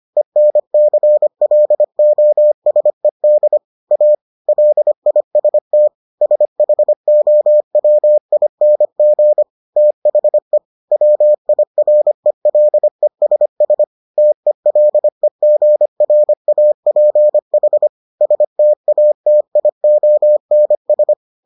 Morse code, heard as a keyed tone, about 25 wpm, 600 Hz. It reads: ENCLOSED A LIST SHOWING THE WIRELESS TELEGRAPH STATIONS